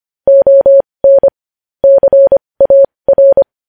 Morse code, heard as a keyed tone, about 25 wpm, spaced as for 20 wpm, 550 Hz. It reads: ON CAR
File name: podcast / News headlines-1x-25x20